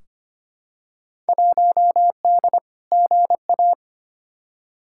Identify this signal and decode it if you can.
Morse code, heard as a keyed tone, about 25 wpm, 700 Hz. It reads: 1D GA
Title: Field Day Exchanges - 25wpm